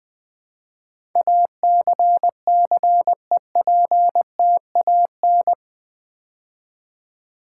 Morse code, heard as a keyed tone, about 20 wpm, 700 Hz. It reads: ACCEPTAN
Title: Build A Word - 4 Letters At A Time - 10-Letter Words - 20wpm